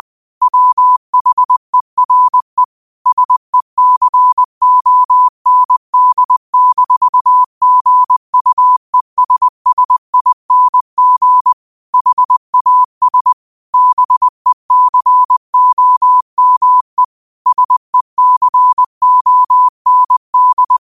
Morse code, heard as a keyed tone, about 20 wpm, 1000 Hz. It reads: WHERE SECOND-GUESSING HAS BECOME SECOND